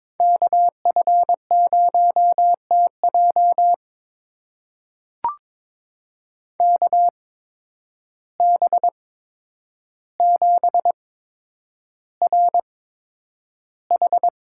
Morse code, heard as a keyed tone, about 21 wpm, spaced as for 9 wpm, 700 Hz. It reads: KF0TJ KB7RH
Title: USA Call Signs - 5 Characters - 8x Character Spacing - 22wpm